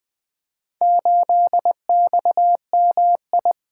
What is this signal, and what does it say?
Morse code, heard as a keyed tone, about 20 wpm, 700 Hz. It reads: 8XMI
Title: Four Letter-Number Combinations - 20wpm